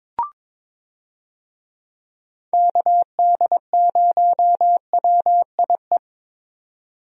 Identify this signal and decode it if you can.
Morse code, heard as a keyed tone, about 22 wpm, 700 Hz. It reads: KD0WIE
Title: USA Call Signs - 6 Characters - Repeated 2x - 22wpm